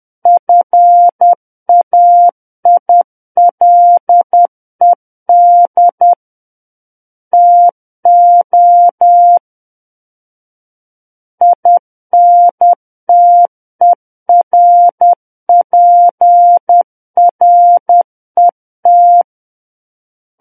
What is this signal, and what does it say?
Morse code, heard as a keyed tone, about 10 wpm, 700 Hz. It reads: FAILED TO INTERPRET